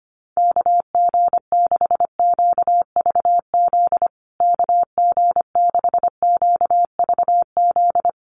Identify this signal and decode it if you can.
Morse code, heard as a keyed tone, about 25 wpm, 700 Hz. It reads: KG6QVZ KG6QVZ
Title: USA Call Signs Repeated 3x 25wpm